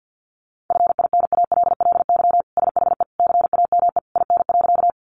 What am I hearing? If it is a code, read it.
Morse code, heard as a keyed tone, about 49 wpm, 700 Hz. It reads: FINALLY SHE CAME IN.
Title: Sentences from Top 1000 Words 50wpm